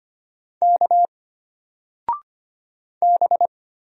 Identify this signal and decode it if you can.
Morse code, heard as a keyed tone, about 25 wpm, 700 Hz. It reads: K B
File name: Single-Letter-Number-25wpm